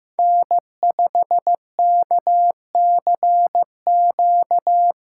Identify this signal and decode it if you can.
Morse code, heard as a keyed tone, about 15 wpm, 700 Hz. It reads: N5KCQ
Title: USA Call Signs - 5 Characters - 3x Character Spacing - 15wpm